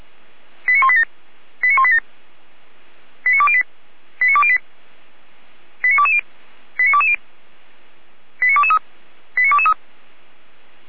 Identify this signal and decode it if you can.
Five-tone selective call (ZVEI), heard as quick runs of five tones, tones of about 70 ms. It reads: 89187 89187 89208 89208 89220 89220 89223 89223